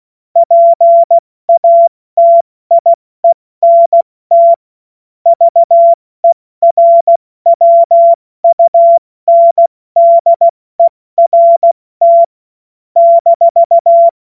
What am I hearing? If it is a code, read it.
Morse code, heard as a keyed tone, about 16 wpm, 680 Hz. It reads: PATIENT VERWUNDERT -